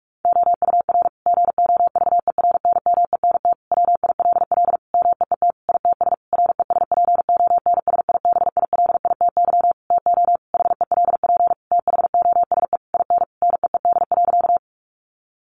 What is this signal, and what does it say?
Morse code, heard as a keyed tone, about 44 wpm, 700 Hz. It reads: OUR GOVERNMENT WILL MEET ITS RESPONSIBILITY TO HELP THOSE IN NEED.
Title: Sentences from Top 1200 Words 45wpm